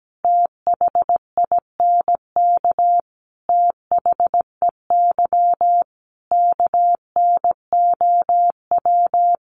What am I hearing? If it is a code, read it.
Morse code, heard as a keyed tone, about 17 wpm, 700 Hz. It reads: THINK THEY KNOW
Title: Sentences from Top 600 Words 17wpm